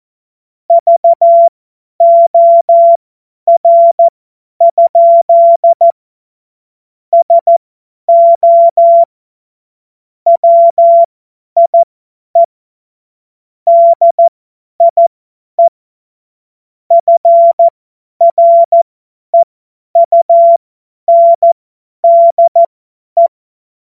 Morse code, heard as a keyed tone, about 14 wpm, spaced as for 10 wpm, 680 Hz. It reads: VOR? SO WIE DIE FREUNDE
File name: Morsekurs / Witze / Witz 26. Oktober 2021, 50 F70